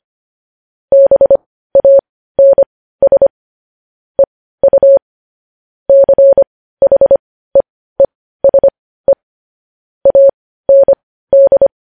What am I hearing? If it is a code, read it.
Morse code, heard as a keyed tone, about 25 wpm, spaced as for 15 wpm, 550 Hz. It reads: BANS EU CHEESE AND
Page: news-headlines-in-morse-code-at-15-wpm